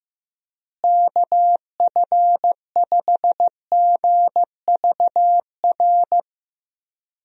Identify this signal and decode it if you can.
Morse code, heard as a keyed tone, about 16 wpm, 700 Hz. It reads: KF5GVR
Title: USA Call Signs - 6 Characters - Repeated 3x - 15wpm